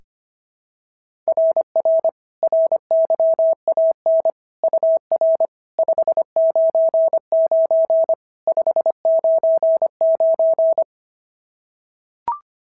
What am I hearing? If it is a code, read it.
Morse code, heard as a keyed tone, about 22 wpm, 650 Hz. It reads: RR RYAN UR 599 599
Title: SOTA QSO Practice 25wpm